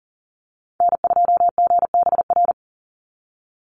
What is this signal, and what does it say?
Morse code, heard as a keyed tone, about 40 wpm, 700 Hz. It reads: N2GBR
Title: NA SOTA Activator Callsigns 40wpm